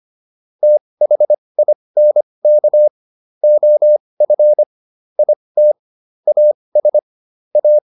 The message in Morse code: THINK OF IT AS A

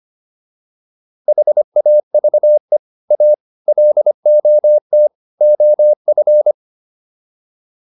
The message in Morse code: HAVE A LOT OF

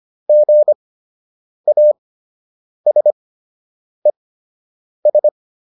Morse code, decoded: GASES

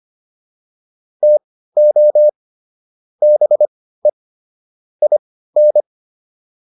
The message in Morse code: TO BE IN

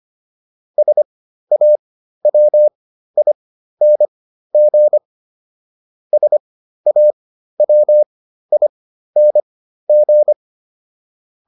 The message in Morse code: SAWING SAWING

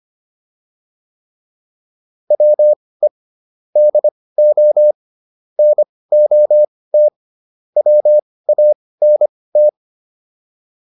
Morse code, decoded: WE DO NOT WANT

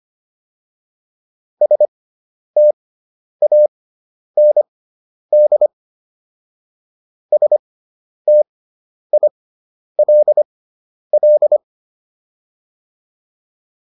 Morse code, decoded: STAND STILL

